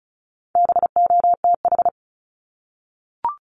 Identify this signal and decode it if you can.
Morse code, heard as a keyed tone, about 34 wpm, 700 Hz. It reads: BOTH
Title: Top 500 Words Repeated 35wpm